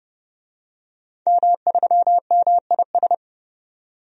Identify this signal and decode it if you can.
Morse code, heard as a keyed tone, about 30 wpm, 700 Hz. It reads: M3MIS
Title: DXCC-100 Call Signs - 5 Characters - 30wpm